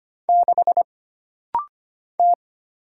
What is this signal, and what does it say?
Morse code, heard as a keyed tone, about 25 wpm, 700 Hz. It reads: 6 T
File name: Single Character (Letter:Number:Punctuation:Prosign) - Rapid-Fire - 25wpm